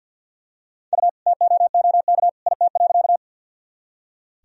Morse code, heard as a keyed tone, about 47 wpm, 700 Hz. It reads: U TOOK IT.
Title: Sentences from Top 300 Words and Abbreviations 50wpm